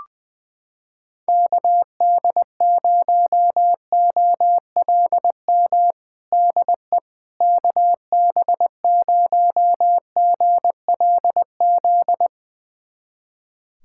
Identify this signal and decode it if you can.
Morse code, heard as a keyed tone, about 21 wpm, 700 Hz. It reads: KD0OLM DE KB0GLZ
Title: USA Callsign DE Callsign - 6 Characters - 20wpm